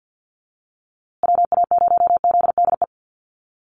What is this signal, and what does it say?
Morse code, heard as a keyed tone, about 48 wpm, 700 Hz. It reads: WA0ZDE
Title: USA Call Signs Repeated 3x 50wpm